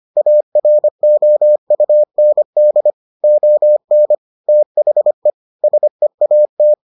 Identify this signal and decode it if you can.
Morse code, heard as a keyed tone, about 25 wpm, 600 Hz. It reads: AROUND ON THE SEAT